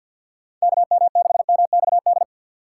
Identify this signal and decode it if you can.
Morse code, heard as a keyed tone, about 50 wpm, 690 Hz. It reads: KM6MXD
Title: USA Call Signs - 6 Characters - 3x Character Spacing - 50wpm